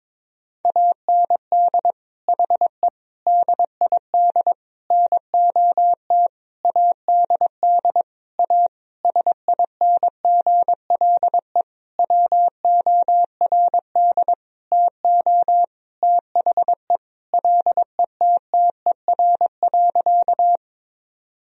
Morse code, heard as a keyed tone, about 22 wpm, 700 Hz. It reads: AND HE DID NOT ADD A SINGLE WORD TO THE LETTER.
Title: Sentences from Top 1000 Words - Review Part 1 - 22wpm